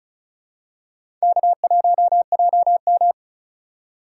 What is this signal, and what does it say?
Morse code, heard as a keyed tone, about 36 wpm, 700 Hz. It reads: K1JM
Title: USA Call Signs - 4 Characters - 2x Character Spacing - 35wpm